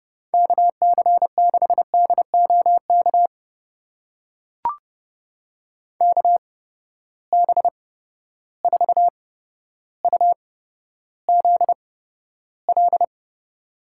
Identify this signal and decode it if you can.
Morse code, heard as a keyed tone, about 30 wpm, spaced as for 13 wpm, 700 Hz. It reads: KC6DOK KB4UZL
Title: USA Call Signs - 6 Characters - 8x Character Spacing - 30wpm